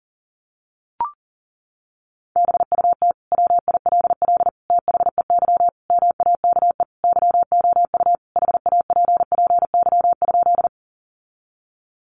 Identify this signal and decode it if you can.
Morse code, heard as a keyed tone, about 40 wpm, 700 Hz. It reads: BUT WILL THEY MAKE YOU HAPPY?
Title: Sentences from Top 1000 Words 40wpm